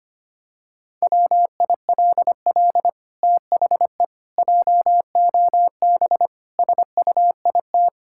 Morse code, heard as a keyed tone, about 25 wpm, 700 Hz. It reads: WILL THE JOB SUIT
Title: Sentences from Top 1600 Words 25wpm